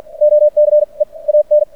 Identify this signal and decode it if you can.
Morse code, heard as a keyed tone, about 50 wpm, spaced as for 35 wpm, 590 Hz. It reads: OCEAN
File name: archived / CRSnet